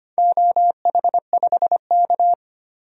Morse code, heard as a keyed tone, about 25 wpm, 700 Hz. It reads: OH5K
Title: DXCC-100 Call Signs - 4 Characters - 25wpm